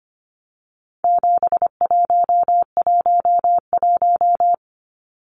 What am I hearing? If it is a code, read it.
Morse code, heard as a keyed tone, about 25 wpm, 700 Hz. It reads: 7111
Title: Numbers - 4-Digit 25wpm